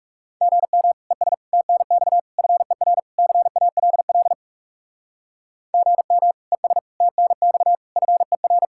Morse code, heard as a keyed tone, about 38 wpm, 700 Hz. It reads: GM ES TNX FER CALL GM ES TNX FER